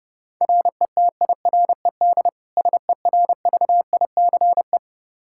Morse code, heard as a keyed tone, about 30 wpm, 700 Hz. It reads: RETIRED SERVICE
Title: K1USN SST QSO Practice 30wpm